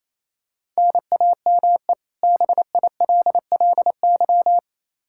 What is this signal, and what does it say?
Morse code, heard as a keyed tone, about 28 wpm, 700 Hz. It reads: NAME BILLY
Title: Practice QSO - Exchange 1 - Beginning - 28wpm